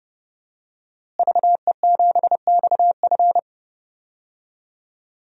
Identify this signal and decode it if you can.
Morse code, heard as a keyed tone, about 30 wpm, 700 Hz. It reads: VE7XF
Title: DXCC-100 Call Signs - 5 Characters - Repeated 2x - 30wpm